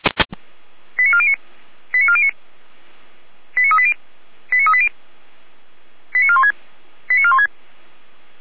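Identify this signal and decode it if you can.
Five-tone selective call (ZVEI), heard as quick runs of five tones, tones of about 70 ms. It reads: 89409 89409 89380 89380 89416 89416
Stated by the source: ZVEI five-tone selective call 89391 received by SDR, from feuerwehr-orlamuende fire brigade audio server.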